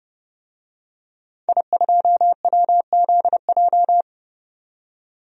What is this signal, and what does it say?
Morse code, heard as a keyed tone, about 31 wpm, 700 Hz. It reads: I2WZJ
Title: DXCC-100 Call Signs - 5 Characters - 30wpm